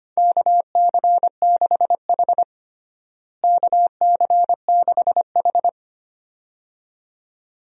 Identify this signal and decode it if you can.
Morse code, heard as a keyed tone, about 25 wpm, 700 Hz. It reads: KC6H KC6H
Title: USA Call Signs - 4 Characters - Repeated 2x - 25wpm